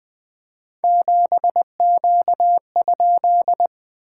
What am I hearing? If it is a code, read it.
Morse code, heard as a keyed tone, about 20 wpm, 690 Hz. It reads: 7Q?
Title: SOTA QSO Practice 20wpm